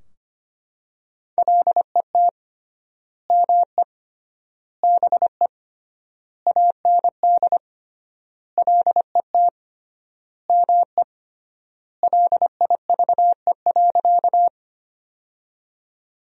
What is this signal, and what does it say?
Morse code, heard as a keyed tone, about 25 wpm, 700 Hz. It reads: LET ME BE AND LET ME LIVE.
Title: 3x Word-Spacing - Sentences from Top 300 Words - 25wpm